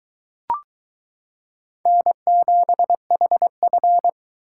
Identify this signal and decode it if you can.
Morse code, heard as a keyed tone, about 22 wpm, 700 Hz. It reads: N7HF